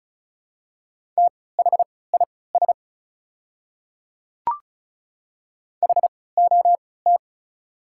Morse code, heard as a keyed tone, about 36 wpm, spaced as for 20 wpm, 690 Hz. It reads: THIS HOT
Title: 3x Character Spacing - Top 100 Words - 35wpm